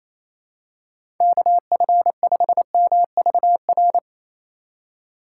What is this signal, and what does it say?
Morse code, heard as a keyed tone, about 28 wpm, 700 Hz. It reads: KF5MVR
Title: USA Call Signs - 3x Character Spacing - 28wpm